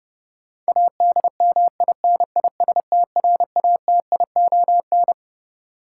Morse code, heard as a keyed tone, about 31 wpm, 700 Hz. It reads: ADMINISTRATION
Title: Sets of 5 Words - New Word Drill - 30wpm